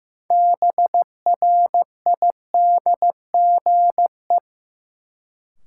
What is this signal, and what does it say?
Morse code, heard as a keyed tone, about 15 wpm, 700 Hz. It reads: BRIDGE
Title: Sentences from Top 1200 Words - New Word Drill Alt - 15wpm